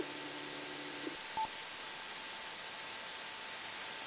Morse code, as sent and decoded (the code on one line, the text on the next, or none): none